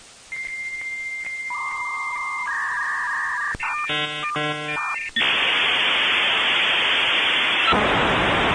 Dial-up modem handshake.